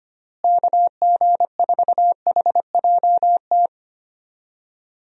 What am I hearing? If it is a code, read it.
Morse code, heard as a keyed tone, about 25 wpm, 700 Hz. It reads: KG4HJT
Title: USA Call Signs - 6 Characters - Repeated 3x - 25wpm